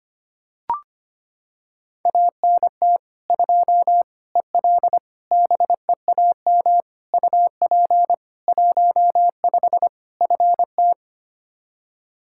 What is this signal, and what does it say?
Morse code, heard as a keyed tone, about 25 wpm, 700 Hz. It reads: ANT 2 EL BEAM UP 15 FT
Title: Practice QSO - Exchange 2 - Info - 25wpm